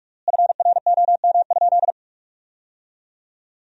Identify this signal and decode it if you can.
Morse code, heard as a keyed tone, about 46 wpm, 700 Hz. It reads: FROM?
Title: 3x Word-Spacing - Sentences from Top 200 Words - 45wpm